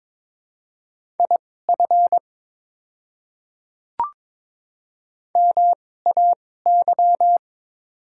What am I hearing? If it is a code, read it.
Morse code, heard as a keyed tone, about 22 wpm, spaced as for 16 wpm, 700 Hz. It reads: IF MAY
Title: 2x Character Spacing - Top 100 Words - 22wpm